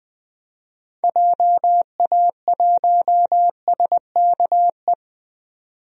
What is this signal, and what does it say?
Morse code, heard as a keyed tone, about 20 wpm, 700 Hz. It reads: JA1SKE
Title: DXCC-100 Call Signs - 6 Characters - 20wpm